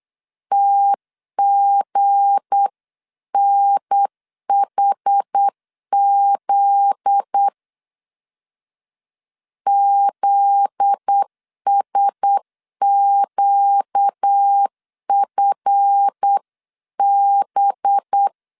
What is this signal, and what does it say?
Morse code, heard as a keyed tone, about 8 wpm, 800 Hz. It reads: TGNHZ ZSQFB
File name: LEZIONI / 7 GF9Z / 7 GF9Z 8wpm 3213